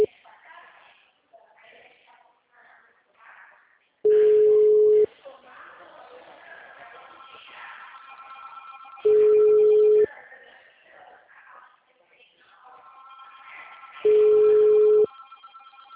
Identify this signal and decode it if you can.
Ringback tone (Europe, 425 Hz), with speech.